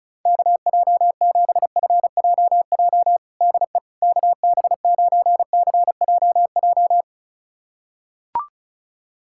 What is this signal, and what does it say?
Morse code, heard as a keyed tone, about 36 wpm, 700 Hz. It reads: KJ7FJJ DE KB9CJJ
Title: USA Callsign DE Callsign - 6 Characters - 35wpm